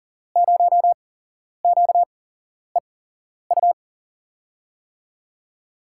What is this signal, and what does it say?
Morse code, heard as a keyed tone, about 39 wpm, spaced as for 11 wpm, 700 Hz. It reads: 0QEU
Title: USA Call Signs - 8x Character Spacing - 40wpm